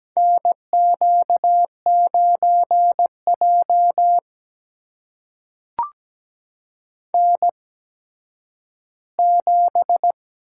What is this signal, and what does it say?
Morse code, heard as a keyed tone, about 17 wpm, spaced as for 10 wpm, 700 Hz. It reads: NQ9J N7
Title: USA Call Signs - 8x Character Spacing - 17wpm